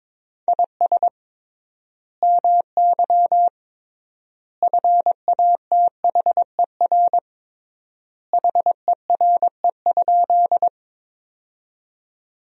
Morse code, heard as a keyed tone, about 22 wpm, 700 Hz. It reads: IS MY FATHER HERE?
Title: 3x Word-Spacing - Sentences from Top 300 Words - 22wpm